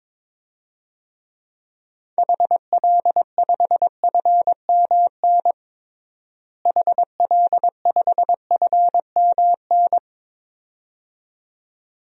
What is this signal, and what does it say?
Morse code, heard as a keyed tone, about 22 wpm, 700 Hz. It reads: HL5FMN HL5FMN
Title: DXCC-100 Call Signs - 6 Characters - Repeated 2x - 22wpm